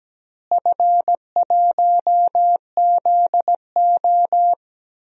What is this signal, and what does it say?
Morse code, heard as a keyed tone, about 17 wpm, 700 Hz. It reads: F1ZO
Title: DXCC-100 Call Signs - 4 Characters - Repeated 2x - 17wpm